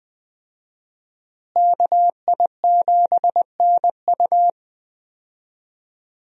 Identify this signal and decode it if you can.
Morse code, heard as a keyed tone, about 20 wpm, 700 Hz. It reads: KI7NU
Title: USA Call Signs - 5 Characters - Repeated 3x - 20wpm